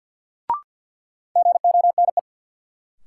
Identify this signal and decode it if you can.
Morse code, heard as a keyed tone, about 46 wpm, 700 Hz. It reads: GONE
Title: Sentences from Top 1200 Words - New Word Drill - 50wpm